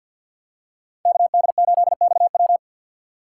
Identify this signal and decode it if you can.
Morse code, heard as a keyed tone, about 46 wpm, 700 Hz. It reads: KD8XW
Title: USA Call Signs - 5 Characters - 2x Character Spacing - 50wpm